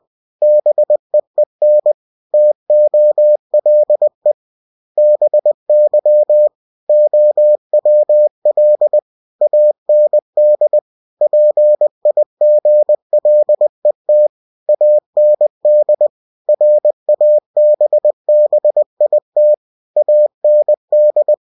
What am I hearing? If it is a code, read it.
Morse code, heard as a keyed tone, about 20 wpm, 600 Hz. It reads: BEEN TOLE BY OWL AND PIGLET AND RABBIT AND
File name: cw-books / winnie-the-pooh / winnie-20wpm-page-0081